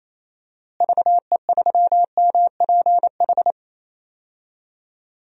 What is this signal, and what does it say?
Morse code, heard as a keyed tone, about 28 wpm, 700 Hz. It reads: VE3MPH
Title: DXCC-100 Call Signs - 6 Characters - Repeated 3x - 28wpm